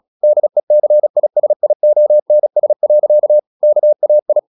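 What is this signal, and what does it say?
Morse code, heard as a keyed tone, about 36 wpm, 600 Hz. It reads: DECISIONS. KAI
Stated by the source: Morse practice at 36 wpm, text of Ken Follett's novel Never, from CW med Gnister.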